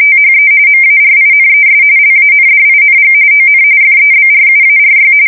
RTTY, about 45.45 baud. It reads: THE MARINES LANDING ON FEBRUARY
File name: fsk170 / USS-New-Jersey